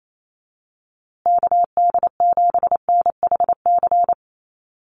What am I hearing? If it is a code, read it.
Morse code, heard as a keyed tone, about 28 wpm, 700 Hz. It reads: KD7NHC
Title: USA Call Signs Repeated 3x 28wpm